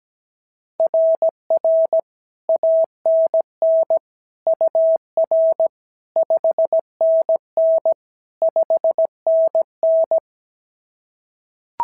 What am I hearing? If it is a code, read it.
Morse code, heard as a keyed tone, about 17 wpm, 690 Hz. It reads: RR ANN UR 5NN 5NN
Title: SOTA QSO Practice 17wpm